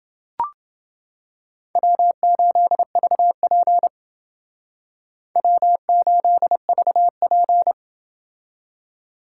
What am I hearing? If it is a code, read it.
Morse code, heard as a keyed tone, about 30 wpm, 700 Hz. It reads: W8VP W8VP